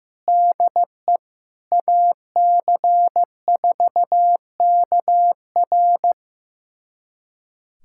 Morse code, heard as a keyed tone, about 16 wpm, 700 Hz. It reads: DE AC4KR
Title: USA Callsign DE Callsign 15wpm